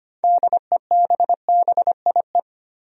Morse code, heard as a keyed tone, about 25 wpm, 700 Hz. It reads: DEBBIE
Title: USA Common Names - 25wpm